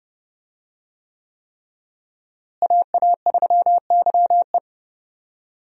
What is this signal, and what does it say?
Morse code, heard as a keyed tone, about 28 wpm, 700 Hz. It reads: AA3YE
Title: USA Call Signs - 5 Characters - Repeated 3x - 30wpm